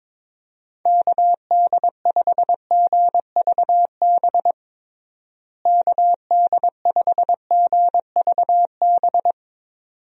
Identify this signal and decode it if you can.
Morse code, heard as a keyed tone, about 23 wpm, 700 Hz. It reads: KD5GVB KD5GVB
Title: USA Call Signs - 6 Characters - Repeated 2x - 22wpm